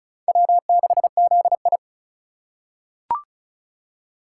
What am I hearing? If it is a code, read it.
Morse code, heard as a keyed tone, about 34 wpm, 700 Hz. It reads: W6ZI
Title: USA Call Signs - 4 Characters - 2x Character Spacing - 35wpm